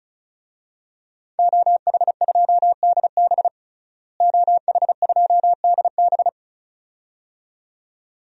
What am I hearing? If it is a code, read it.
Morse code, heard as a keyed tone, about 34 wpm, 700 Hz. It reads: OH2DB OH2DB
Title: DXCC-100 Call Signs - 5 Characters - Repeated 2x - 35wpm